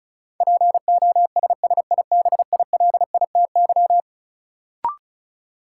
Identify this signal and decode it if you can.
Morse code, heard as a keyed tone, about 34 wpm, 700 Hz. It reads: POSSIBILITY
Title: Sets of 3 Words - New Word Drill - 35wpm